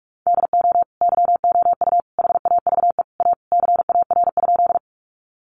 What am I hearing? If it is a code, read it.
Morse code, heard as a keyed tone, about 44 wpm, 700 Hz. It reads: DO YOU HAVE A CAR?
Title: Sentences from Top 300 Words 45wpm